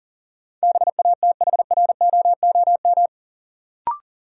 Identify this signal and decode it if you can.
Morse code, heard as a keyed tone, about 38 wpm, 700 Hz. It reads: BATHROOM
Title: Sets of 5 Words Encore - New Word Drill - 40wpm